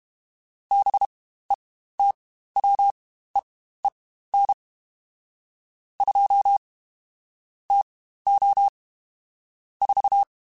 Morse code, heard as a keyed tone, about 32 wpm, spaced as for 15 wpm, 800 Hz. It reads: BETWEEN 2 TO 4
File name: Spaced Characters / Spaced Characters-mp3s / Weather 4 14 sp @30